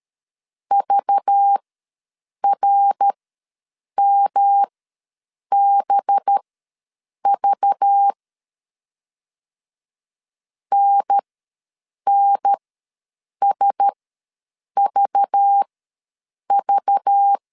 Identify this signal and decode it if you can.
Morse code, heard as a keyed tone, about 13 wpm, spaced as for 7 wpm, 800 Hz. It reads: VRMBV NNSVV
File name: LEZIONI / 5 LBVC / 5 LBV ws 12wpm 3015